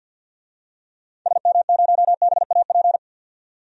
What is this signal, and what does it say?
Morse code, heard as a keyed tone, about 53 wpm, 700 Hz. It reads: SM0BAP
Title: DXCC-100 Call Signs - 6 Characters - 50wpm